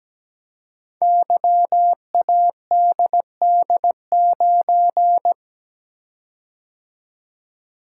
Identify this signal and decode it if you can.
Morse code, heard as a keyed tone, about 17 wpm, 700 Hz. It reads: YADD9
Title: Code Groups - 5 Letter:Numbers - 17wpm